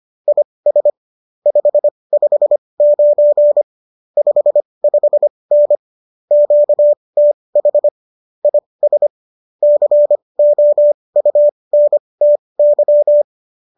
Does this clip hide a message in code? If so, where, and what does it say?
Yes: Morse code from 0.3 to 13.2 s, heard as a keyed tone, about 25 wpm, spaced as for 20 wpm, 590 Hz. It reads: IS 559 55N QTH IS COUNTY